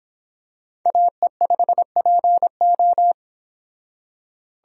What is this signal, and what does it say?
Morse code, heard as a keyed tone, about 26 wpm, 700 Hz. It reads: AE5PO